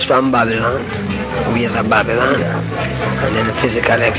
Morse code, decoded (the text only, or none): none